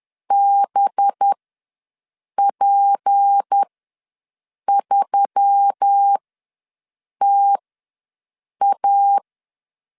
Morse code, heard as a keyed tone, about 10 wpm, spaced as for 6 wpm, 800 Hz. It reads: BP3TA
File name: LEZIONI / 6 PKX0 / 6 P ws 10wpm 3751